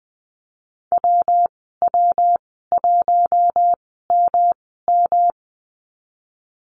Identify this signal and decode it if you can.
Morse code, heard as a keyed tone, about 20 wpm, spaced as for 14 wpm, 700 Hz. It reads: WW1MM
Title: 154 Lesson 25g - Letter p - Call Signs 20wpm